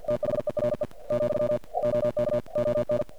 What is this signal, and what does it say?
Morse code, heard as a keyed tone, about 45 wpm, 590 Hz. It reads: THERE , OK ON
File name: archived / CRSnet